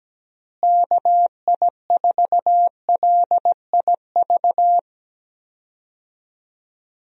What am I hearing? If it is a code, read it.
Morse code, heard as a keyed tone, about 17 wpm, 700 Hz. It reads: KI4LIV
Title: USA Call Signs - 6 Characters - Repeated 3x - 17wpm